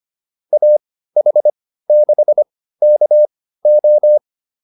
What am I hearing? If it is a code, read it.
Morse code, heard as a keyed tone, about 25 wpm, spaced as for 15 wpm, 600 Hz. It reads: AH6KO